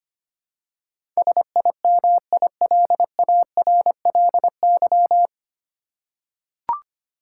Morse code, heard as a keyed tone, about 25 wpm, 700 Hz. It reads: SIMILARLY
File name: practice / 25wpm / 9 Letter Words - First 3 Letters Given - 25wpm